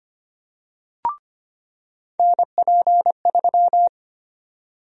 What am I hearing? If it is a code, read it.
Morse code, heard as a keyed tone, about 25 wpm, 700 Hz. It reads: NP3
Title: USA Call Sign Prefixes - 25wpm